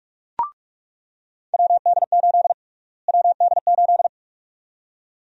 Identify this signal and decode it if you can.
Morse code, heard as a keyed tone, about 46 wpm, 700 Hz. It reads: WD8 WD8
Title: USA Call Sign Prefixes - Repeated 2x - 45wpm